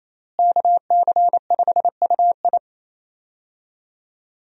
Morse code, heard as a keyed tone, about 28 wpm, 700 Hz. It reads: KC5UI
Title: USA Call Signs - 5 Characters - Repeated 3x - 28wpm